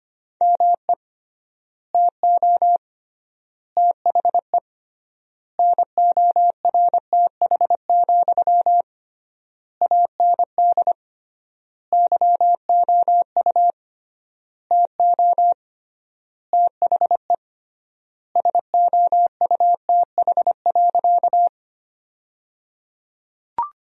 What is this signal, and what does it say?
Morse code, heard as a keyed tone, about 25 wpm, 700 Hz. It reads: ME TO THE NORTH, AND YOU TO THE SOUTH.
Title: 3x Word-Spacing - Sentences from Top 400 Words - 25wpm